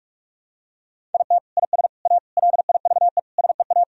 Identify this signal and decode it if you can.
Morse code, heard as a keyed tone, about 44 wpm, 700 Hz. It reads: IT IS A LIVE SEA
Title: Sentences from Top 1000 Words - Review Part 1 - 45wpm